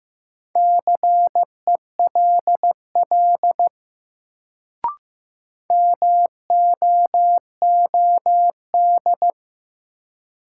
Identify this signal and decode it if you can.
Morse code, heard as a keyed tone, about 15 wpm, 700 Hz. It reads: CELL MOOD
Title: Sentences from Top 1100 Words - New Word Drill Alt - 15wpm